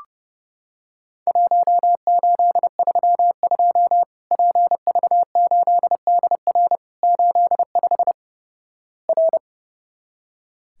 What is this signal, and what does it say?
Morse code, heard as a keyed tone, about 28 wpm, 700 Hz. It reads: 1832 PV8DR 85 R
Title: Worked All Europe DX Contest - QTC Exchange - 30wpm